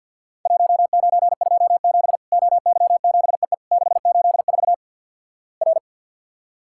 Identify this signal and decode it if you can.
Morse code, heard as a keyed tone, about 49 wpm, 700 Hz. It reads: 1927 OY7EE 684 R
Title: Worked All Europe DX Contest - QTC Exchange with Repeats - 50wpm